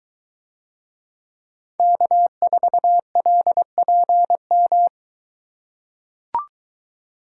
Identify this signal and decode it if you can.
Morse code, heard as a keyed tone, about 22 wpm, 700 Hz. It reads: K4LPM